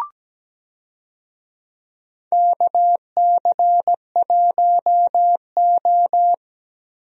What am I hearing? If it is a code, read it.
Morse code, heard as a keyed tone, about 17 wpm, 700 Hz. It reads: KC1O
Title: USA Call Signs - 4 Characters - Repeated 2x - 17wpm